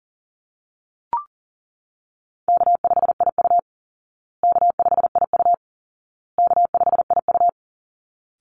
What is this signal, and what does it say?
Morse code, heard as a keyed tone, about 40 wpm, 700 Hz. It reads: K5IU K5IU K5IU
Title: Call Signs Repeated 2x - 40wpm